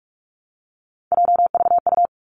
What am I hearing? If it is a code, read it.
Morse code, heard as a keyed tone, about 46 wpm, 700 Hz. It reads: JVU
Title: Three Challenging Letter-Number Combinations 45wpm